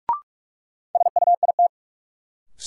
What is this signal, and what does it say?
Morse code, heard as a keyed tone, about 46 wpm, 700 Hz. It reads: SUIT
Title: Sentences from Top 1600 Words - New Word Drill Alt - 45wpm